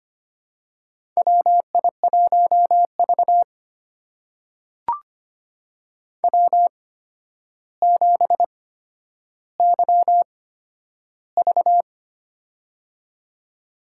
Morse code, heard as a keyed tone, about 25 wpm, spaced as for 11 wpm, 700 Hz. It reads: WI1V W7YV
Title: USA Call Signs - 4 Characters - 8x Character Spacing - 25wpm